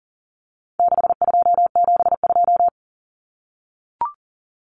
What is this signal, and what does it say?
Morse code, heard as a keyed tone, about 39 wpm, 700 Hz. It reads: 6272
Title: Numbers - 4-Digit 40wpm